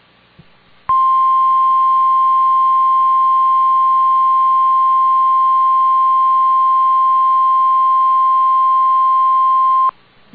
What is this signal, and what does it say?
Emergency Alert System (SAME) tones (the 1050 Hz attention tone). It is heard from 0.8 to 10.0 s.